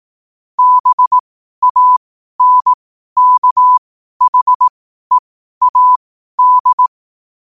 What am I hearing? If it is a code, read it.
Morse code, heard as a keyed tone, about 18 wpm, spaced as for 12 wpm, 1000 Hz. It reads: BANKHEAD
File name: qotdfiles / 12wpm